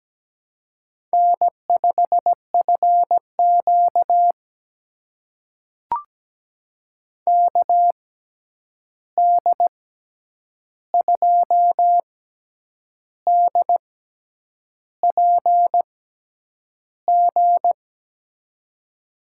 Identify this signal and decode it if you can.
Morse code, heard as a keyed tone, about 17 wpm, spaced as for 8 wpm, 700 Hz. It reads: N5FQ KD2DPG
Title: USA Call Signs - 6x Character Spacing - 17wpm